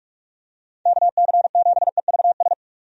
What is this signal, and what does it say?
Morse code, heard as a keyed tone, about 42 wpm, 700 Hz. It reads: KC7EVS
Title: USA Call Signs - 6 Characters - 45wpm